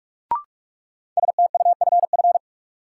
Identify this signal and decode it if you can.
Morse code, heard as a keyed tone, about 46 wpm, 700 Hz. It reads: STUFF
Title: Sentences from Top 1400 Words - New Word Drill - 45wpm